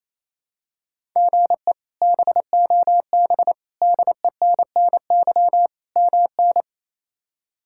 Morse code, heard as a keyed tone, about 28 wpm, 700 Hz. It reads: GE BOB DENNY MN
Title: K1USN SST QSO Practice - 28wpm